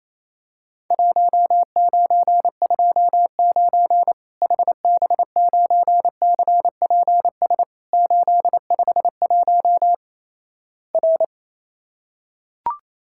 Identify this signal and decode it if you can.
Morse code, heard as a keyed tone, about 28 wpm, 700 Hz. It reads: 1929 HB9CPS 851 R
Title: Worked All Europe DX Contest - QTC Exchange - 28wpm